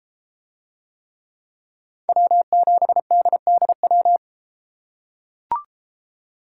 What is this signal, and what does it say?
Morse code, heard as a keyed tone, about 31 wpm, 700 Hz. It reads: W7DDW